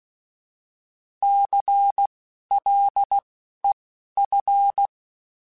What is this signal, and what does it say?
Morse code, heard as a keyed tone, about 16 wpm, spaced as for 11 wpm, 800 Hz. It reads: CLEF
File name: Alphabetical / Alphabetical mp3s / Cs @10